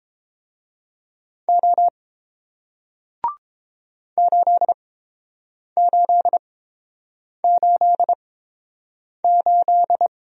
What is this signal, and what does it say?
Morse code, heard as a keyed tone, about 30 wpm, spaced as for 7 wpm, 700 Hz. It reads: O 8888